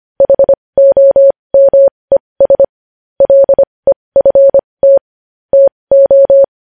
Morse code, heard as a keyed tone, about 25 wpm, spaced as for 20 wpm, 550 Hz. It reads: HOMES LEFT TO
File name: podcast / News headlines-1x-25x20